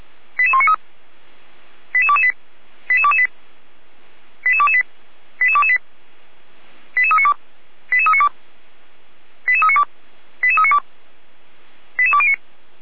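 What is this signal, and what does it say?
Five-tone selective call (ZVEI), heard as quick runs of five tones, tones of about 70 ms. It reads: 88193 88228 88228 88228 88228 88392 88392 88392 88392 88209
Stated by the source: ZVEI five-tone call received by SDR, from the Feuerwehr Orlamünde fire brigade server.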